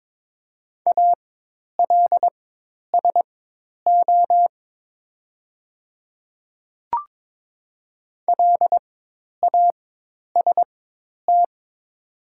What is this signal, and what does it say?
Morse code, heard as a keyed tone, about 22 wpm, spaced as for 10 wpm, 700 Hz. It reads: ALSO LAST
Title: 4x Character Spacing - Top 300 Words - 22wpm